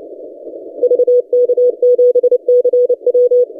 Morse code, heard as a keyed tone, about 29 wpm, 490 Hz. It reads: VK7CW
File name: posts / 3d2rh-20cw- -0330z